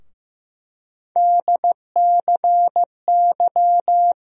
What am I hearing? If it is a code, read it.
Morse code, heard as a keyed tone, about 15 wpm, 700 Hz. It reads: DCY
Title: USA Call Sign Postfixes - 3 Letters 15wpm